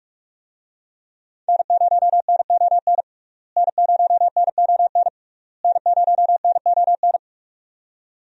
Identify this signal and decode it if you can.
Morse code, heard as a keyed tone, about 46 wpm, 700 Hz. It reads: N0NON N0NON N0NON
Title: USA Call Signs - 5 Characters - Repeated 3x - 45wpm